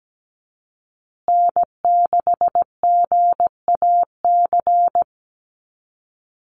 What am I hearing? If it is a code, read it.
Morse code, heard as a keyed tone, about 17 wpm, 700 Hz. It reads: N6GAC
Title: USA Call Signs Repeated 3x 17wpm